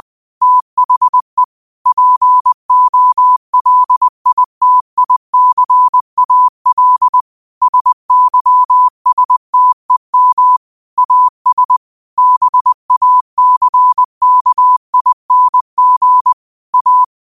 Morse code, heard as a keyed tone, about 20 wpm, 1000 Hz. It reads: THE POLITICAL SYSTEM AS BACKING A